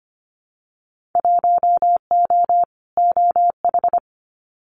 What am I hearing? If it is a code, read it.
Morse code, heard as a keyed tone, about 25 wpm, 700 Hz. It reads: 1O OH
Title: Winter Field Day Exchanges 25wpm